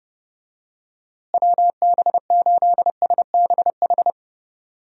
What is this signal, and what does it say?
Morse code, heard as a keyed tone, about 28 wpm, 700 Hz. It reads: WB8SBH
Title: USA Call Signs - 6 Characters - Repeated 3x - 30wpm